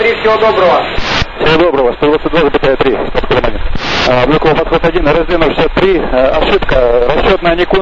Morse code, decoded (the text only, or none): none